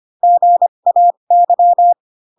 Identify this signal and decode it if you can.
Morse code, heard as a keyed tone, about 25 wpm, spaced as for 22 wpm, 700 Hz. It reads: GAY